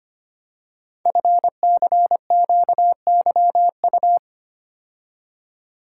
Morse code, heard as a keyed tone, about 25 wpm, 700 Hz. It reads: FCQYU
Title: Code Groups - 5 Letters - 25wpm